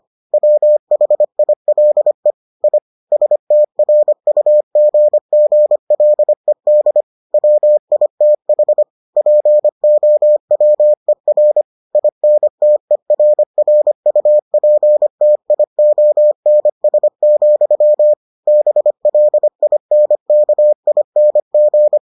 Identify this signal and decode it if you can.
Morse code, heard as a keyed tone, about 25 wpm, 600 Hz. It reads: WHILE I STRUGGLED WITH POWER INTERRUPTIONS, BLINKING